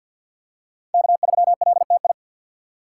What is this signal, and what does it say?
Morse code, heard as a keyed tone, about 51 wpm, 700 Hz. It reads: K3LTI
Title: USA Call Signs - 2x Character Spacing - 50wpm